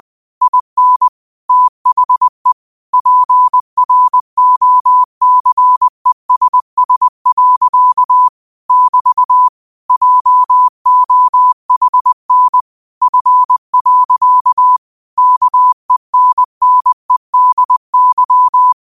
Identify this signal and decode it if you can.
Morse code, heard as a keyed tone, about 20 wpm, 1000 Hz. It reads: IN THE PROCESS. <BT> JOHN F. KENNEDY
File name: qotdfiles / 20wpm